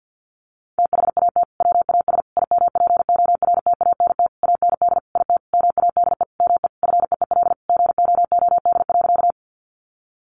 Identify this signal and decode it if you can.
Morse code, heard as a keyed tone, about 50 wpm, 700 Hz. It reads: THAT WAS IMPORTANT AND IT MADE ME FEEL GOOD.
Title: Sentences from Top 600 Words 50wpm